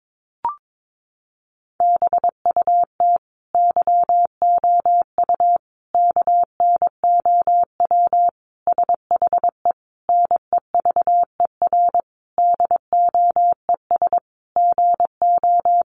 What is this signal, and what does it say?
Morse code, heard as a keyed tone, about 22 wpm, 700 Hz. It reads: BUT YOU KNOW SHE NEVER DOES GO
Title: Sentences from Top 1400 Words 22wpm